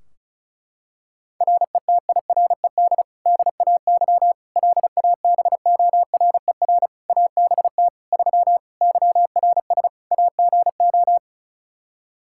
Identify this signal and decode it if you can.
Morse code, heard as a keyed tone, about 36 wpm, 700 Hz. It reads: RETIRED DAY LABORER ABT 3 YRS AGO
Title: Practice QSO - Exchange 3 - More Info - 35wpm